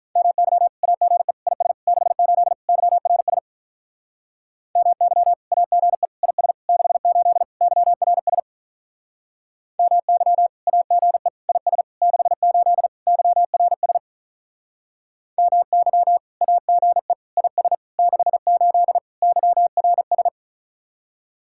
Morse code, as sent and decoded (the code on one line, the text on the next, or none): -- -.-- / .- --. / .. ... / -.... ---.. / -.-- .-. ... / -- -.-- / .- --. / .. ... / -.... ---.. / -.-- .-. ... / -- -.-- / .- --. / .. ... / -.... ---.. / -.-- .-. ... / -- -.-- / .- --. / .. ... / -.... ---.. / -.-- .-. ...
MY AGE IS 68 YRS MY AGE IS 68 YRS MY AGE IS 68 YRS MY AGE IS 68 YRS